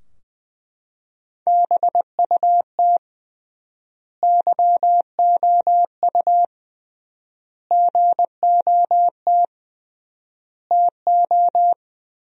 Morse code, heard as a keyed tone, about 20 wpm, 700 Hz. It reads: BUT YOU GOT TO
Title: 3x Word-Spacing - Sentences from Top 300 Words - 20wpm